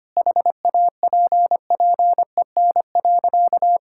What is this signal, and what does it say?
Morse code, heard as a keyed tone, about 25 wpm, 700 Hz. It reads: HAPPEN.
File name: practice / 25wpm / Sentences with 9-Word Phrases - 25wpm